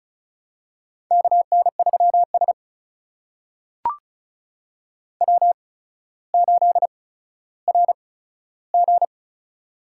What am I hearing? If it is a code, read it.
Morse code, heard as a keyed tone, about 36 wpm, spaced as for 15 wpm, 700 Hz. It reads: KN3S W8RG
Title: USA Call Signs - 4 Characters - 8x Character Spacing - 35wpm